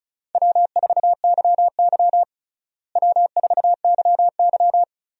Morse code, heard as a keyed tone, about 36 wpm, 700 Hz. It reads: W4YY W4YY
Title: USA Call Signs - 4 Characters - Repeated 3x - 35wpm